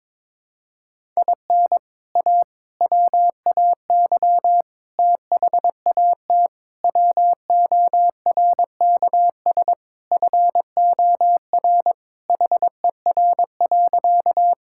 Morse code, heard as a keyed tone, about 22 wpm, 700 Hz. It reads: IN A WAY THAT WORKS FOR HER.
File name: practice / 22wpm / Sentences with 4-Word Phrases - 22wpm